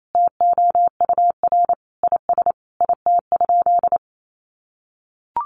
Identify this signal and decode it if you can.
Morse code, heard as a keyed tone, about 28 wpm, 700 Hz. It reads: TOUR IS IT?
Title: Sentences from Top 1400 Words 28wpm